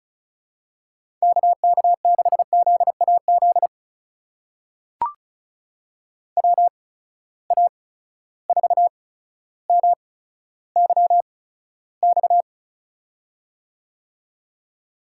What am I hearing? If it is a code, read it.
Morse code, heard as a keyed tone, about 36 wpm, spaced as for 15 wpm, 700 Hz. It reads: KK6ZAZ WA4MYX
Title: USA Call Signs - 6 Characters - 8x Character Spacing - 35wpm